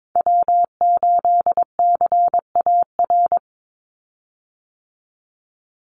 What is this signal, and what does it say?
Morse code, heard as a keyed tone, about 22 wpm, 700 Hz. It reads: W8CAR
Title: Call Signs Repeated 3x - 22wpm